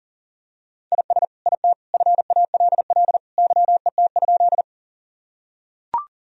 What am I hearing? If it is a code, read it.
Morse code, heard as a keyed tone, about 37 wpm, 700 Hz. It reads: IS IT FALL YET?
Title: Sentences from Top 1000 Words - Review Part 2 - 40wpm